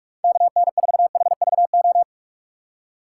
Morse code, heard as a keyed tone, about 42 wpm, 700 Hz. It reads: KN4HVO
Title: USA Call Signs - 6 Characters - 4x Character Spacing - 45wpm